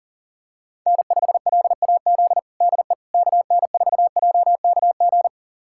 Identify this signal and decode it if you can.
Morse code, heard as a keyed tone, about 40 wpm, 700 Hz. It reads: N5LAZ DE KN4JKG
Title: USA Callsign DE Callsign 40wpm